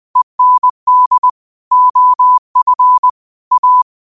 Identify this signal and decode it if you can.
Morse code, heard as a keyed tone, about 20 wpm, 1000 Hz. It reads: END OF A
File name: qotdfiles / 20wpm